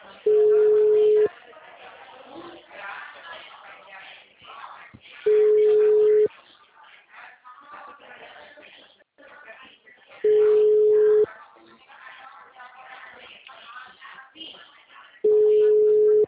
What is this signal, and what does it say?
Ringback tone (Europe, 425 Hz), with speech.